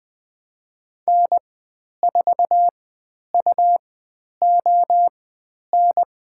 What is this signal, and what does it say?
Morse code, heard as a keyed tone, about 20 wpm, spaced as for 10 wpm, 700 Hz. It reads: N4UON
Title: 102 Lesson 17g - Letter u - Call Signs 20wpm